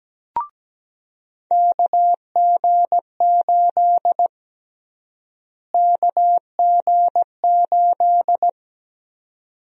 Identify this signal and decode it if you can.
Morse code, heard as a keyed tone, about 17 wpm, 700 Hz. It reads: KG8 KG8
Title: USA Call Sign Prefixes - Repeated 2x - 17wpm